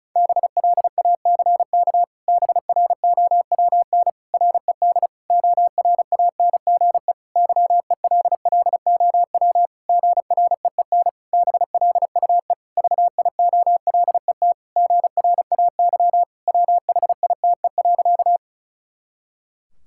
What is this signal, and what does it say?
Morse code, heard as a keyed tone, about 34 wpm, 700 Hz. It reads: BLACK BROWN RED ORANGE YELLOW GREEN BLUE VIOLET GRAY WHITE.
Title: Pangram Challenge v2 - 35wpm